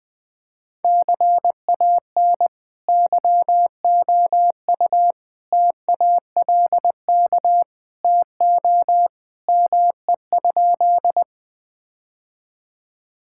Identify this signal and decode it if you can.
Morse code, heard as a keyed tone, about 20 wpm, 700 Hz. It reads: CAN YOU TALK TO ME?
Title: Sentences with 2-Word Phrases - Spoken Beginning - 20wpm